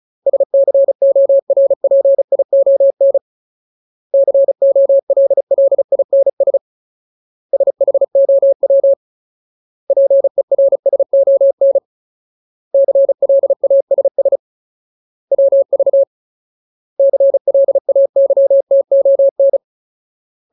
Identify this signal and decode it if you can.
Morse code, heard as a keyed tone, about 34 wpm, 550 Hz. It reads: SCORPION COLLINS SHOW PERSON CLASS WV CLAYTON
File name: Ultimate Word List / 35WPM Ultimate